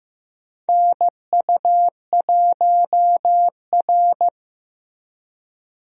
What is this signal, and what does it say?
Morse code, heard as a keyed tone, about 15 wpm, 700 Hz. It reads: NU1R
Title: USA Call Signs - 4 Characters - Repeated 3x - 15wpm